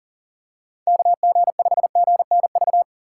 Morse code, heard as a keyed tone, about 40 wpm, 700 Hz. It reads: KG5GNV
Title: USA Call Signs - 6 Characters - 3x Character Spacing - 40wpm